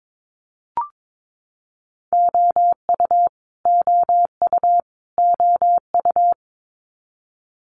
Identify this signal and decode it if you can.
Morse code, heard as a keyed tone, about 22 wpm, 700 Hz. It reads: OU OU OU
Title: Multi-letter Phonemes Repeated 3x 22wpm